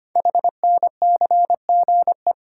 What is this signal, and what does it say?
Morse code, heard as a keyed tone, about 25 wpm, 700 Hz. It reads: HNCGE
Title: Code Groups - 5 Letters - 25wpm